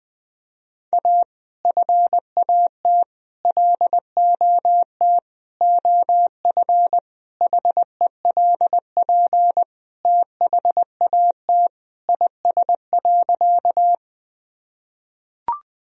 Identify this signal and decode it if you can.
Morse code, heard as a keyed tone, about 20 wpm, 700 Hz. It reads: A FAT LOT OF HELP THAT IS.
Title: Sentences from Top 1000 Words - Review Part 4 - 20wpm